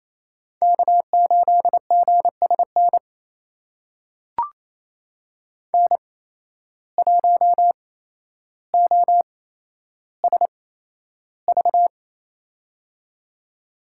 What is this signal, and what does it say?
Morse code, heard as a keyed tone, about 28 wpm, spaced as for 12 wpm, 700 Hz. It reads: K8GSN N1OSV
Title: USA Call Signs - 5 Characters - 8x Character Spacing - 28wpm